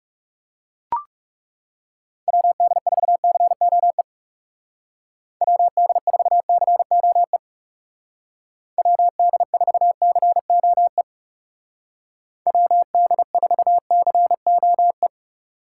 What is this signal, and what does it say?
Morse code, heard as a keyed tone, about 36 wpm, 700 Hz. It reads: WD4COE WD4COE WD4COE WD4COE
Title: Speed Racing - USA Call Signs 30wpm